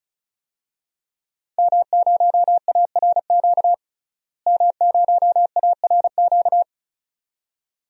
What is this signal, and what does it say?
Morse code, heard as a keyed tone, about 36 wpm, 700 Hz. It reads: M0ARQ M0ARQ
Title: DXCC-100 Call Signs - 5 Characters - Repeated 2x - 35wpm